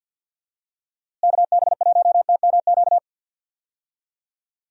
Morse code, heard as a keyed tone, about 51 wpm, 700 Hz. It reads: KB1TMQ
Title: USA Call Signs - 6 Characters - Repeated 2x - 50wpm